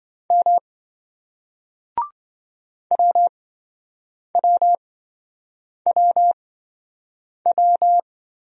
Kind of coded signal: Morse code

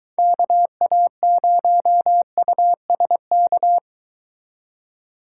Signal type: Morse code

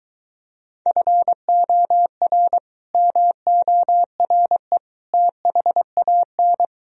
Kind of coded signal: Morse code

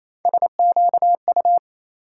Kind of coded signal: Morse code

